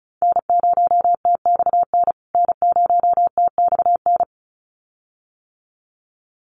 Morse code, heard as a keyed tone, about 34 wpm, 700 Hz. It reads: N0TXN N0TXN
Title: USA Call Signs Repeated 2x 35wpm